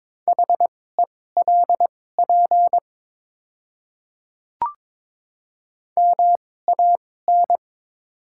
Morse code, heard as a keyed tone, about 22 wpm, spaced as for 16 wpm, 700 Hz. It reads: HELP MAN
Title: 2x Character Spacing - Top 300 Words - 22wpm